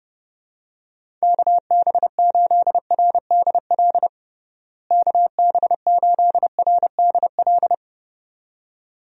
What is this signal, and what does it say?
Morse code, heard as a keyed tone, about 32 wpm, 700 Hz. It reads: KB8RDL KB8RDL
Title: USA Call Signs - 6 Characters - Repeated 2x - 30wpm